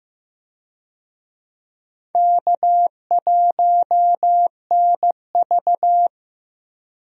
Morse code, heard as a keyed tone, about 15 wpm, 700 Hz. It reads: K1NV
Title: USA Call Signs - 4 Characters - Repeated 3x - 15wpm